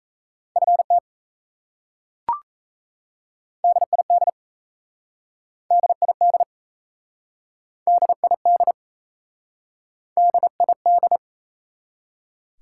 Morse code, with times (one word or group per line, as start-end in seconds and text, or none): 0.6-1.0 FT
3.6-4.3 DID
5.7-6.4 DID
7.9-8.7 DID
10.2-11.2 DID